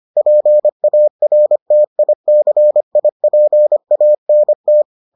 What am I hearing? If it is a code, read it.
Morse code, heard as a keyed tone, about 25 wpm, 600 Hz. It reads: PARTICIPANT